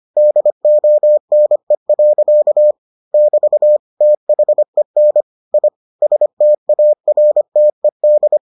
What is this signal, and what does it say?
Morse code, heard as a keyed tone, about 25 wpm, 600 Hz. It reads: DONE. <BT> THEN I STARTED